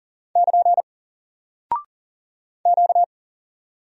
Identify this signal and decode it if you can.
Morse code, heard as a keyed tone, about 40 wpm, 700 Hz. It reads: <KN> Q
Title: Single Character (Letter:Number:Punctuation:Prosign) Alt 40wpm